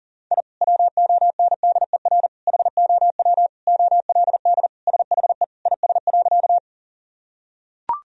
Morse code, heard as a keyed tone, about 40 wpm, 700 Hz. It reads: I WONDER HOW OLD SHE IS.
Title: Sentences from Top 1000 Words - Review Part 3 - 40wpm